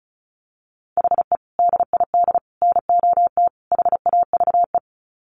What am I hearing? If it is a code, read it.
Morse code, heard as a keyed tone, about 34 wpm, 700 Hz. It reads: HE DID NOT HAVE
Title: Sets of 4 Words 35wpm